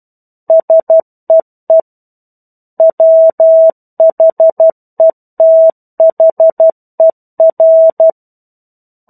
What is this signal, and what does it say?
Morse code, heard as a keyed tone, about 12 wpm, 650 Hz. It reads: SEE WHETHER
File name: 01-Chapter-01 12 WPM